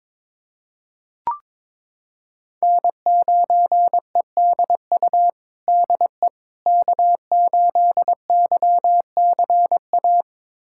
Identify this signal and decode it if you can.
Morse code, heard as a keyed tone, about 21 wpm, 700 Hz. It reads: N9EDU DE K8YCA
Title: USA Callsign DE Callsign - 5 Characters - 22wpm